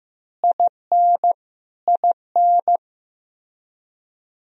Morse code, heard as a keyed tone, about 15 wpm, 700 Hz. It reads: IN IN
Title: Top 100 Words Repeated 15wpm